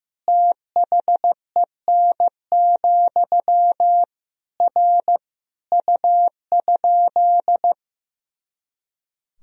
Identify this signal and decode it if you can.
Morse code, heard as a keyed tone, about 16 wpm, 700 Hz. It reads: THEN, R U?
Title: Sentences from Top 100 Words and Abbreviations 15wpm